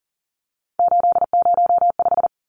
Morse code, heard as a keyed tone, about 40 wpm, 700 Hz. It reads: 805
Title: Numbers - 3-Digit 40wpm